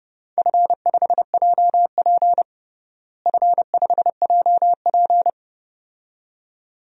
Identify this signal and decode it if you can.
Morse code, heard as a keyed tone, about 32 wpm, 700 Hz. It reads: F5JP F5JP
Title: DXCC-100 Call Signs - 4 Characters - Repeated 2x - 30wpm